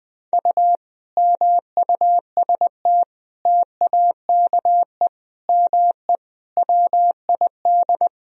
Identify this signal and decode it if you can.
Morse code, heard as a keyed tone, about 20 wpm, 700 Hz. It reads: U MUST TAKE ME WID
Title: Sentences from Top 200 Words and Abbreviations 20wpm